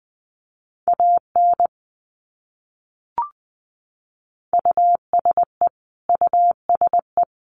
Morse code, heard as a keyed tone, about 20 wpm, 700 Hz. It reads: AN USE USE
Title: Top 200 Words Repeated 20wpm